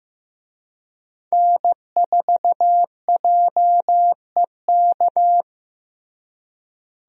Morse code, heard as a keyed tone, about 15 wpm, 700 Hz. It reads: N4JEK
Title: USA Call Signs - 5 Characters - Repeated 2x - 15wpm